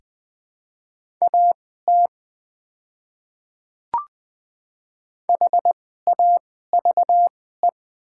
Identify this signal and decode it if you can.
Morse code, heard as a keyed tone, about 20 wpm, spaced as for 15 wpm, 700 Hz. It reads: AT HAVE
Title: 2x Character Spacing - Top 100 Words - 20wpm